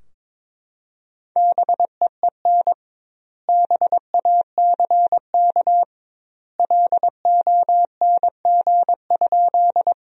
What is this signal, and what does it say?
Morse code, heard as a keyed tone, about 22 wpm, 700 Hz. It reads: BEEN BACK LONG?
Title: 2x Word-Spacing - Sentences from Top 200 Words - 22wpm